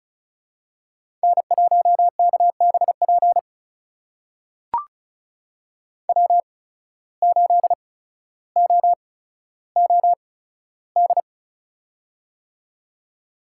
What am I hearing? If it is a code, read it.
Morse code, heard as a keyed tone, about 34 wpm, spaced as for 15 wpm, 700 Hz. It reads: N1KBP W8OOD
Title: USA Call Signs - 5 Characters - 8x Character Spacing - 35wpm